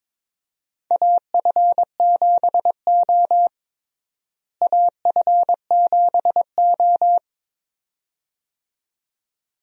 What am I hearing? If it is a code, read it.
Morse code, heard as a keyed tone, about 22 wpm, 700 Hz. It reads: AF7O AF7O
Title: USA Call Signs - 4 Characters - Repeated 2x - 22wpm